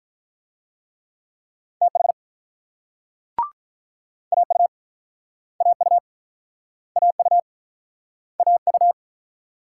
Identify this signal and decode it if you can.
Morse code, heard as a keyed tone, about 44 wpm, 700 Hz. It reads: TH AU AU AU AU
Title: Speed-Racing Multi-letter Phonemes - 35wpm